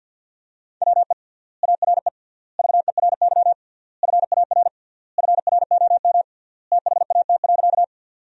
Morse code, heard as a keyed tone, about 50 wpm, 700 Hz. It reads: WE ARE VERY FAR FROM THAT.
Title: 3x Word-Spacing - Sentences from Top 300 Words - 50wpm